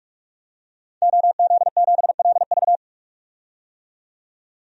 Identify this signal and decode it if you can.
Morse code, heard as a keyed tone, about 46 wpm, 700 Hz. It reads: OZ7LV
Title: DXCC-100 Call Signs - 5 Characters - 45wpm